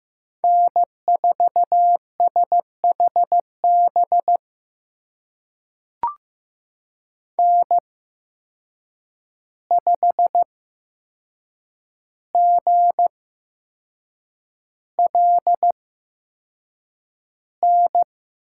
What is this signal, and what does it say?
Morse code, heard as a keyed tone, about 15 wpm, spaced as for 6 wpm, 700 Hz. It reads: N4SHB N5GLN
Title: USA Call Signs - 5 Characters - 8x Character Spacing - 15wpm